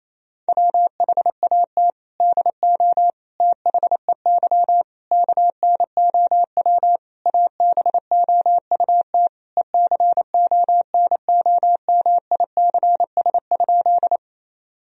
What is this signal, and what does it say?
Morse code, heard as a keyed tone, about 28 wpm, 700 Hz. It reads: WHAT DO THEY KNOW ABOUT ECONOMICS?
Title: Sentences from Top 1000 Words - Review Part 3 - 28wpm